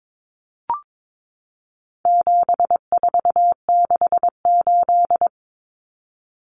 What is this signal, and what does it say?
Morse code, heard as a keyed tone, about 22 wpm, 700 Hz. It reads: 7468
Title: Numbers - 4-Digit 22wpm